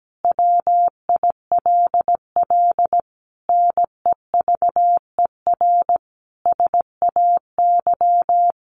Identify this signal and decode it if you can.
Morse code, heard as a keyed tone, about 17 wpm, 700 Hz. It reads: WILL NEVER SAY
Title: Sentences from Top 1400 Words 17wpm